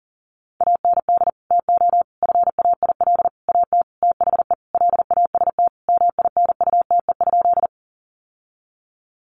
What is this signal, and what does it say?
Morse code, heard as a keyed tone, about 40 wpm, 700 Hz. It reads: AND TO FAIL AT THE LAST MINUTE?
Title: Sentences from Top 1100 Words 40wpm